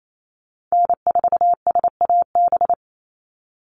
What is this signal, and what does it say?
Morse code, heard as a keyed tone, about 28 wpm, 700 Hz. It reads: N4SAB
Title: USA Call Signs 28wpm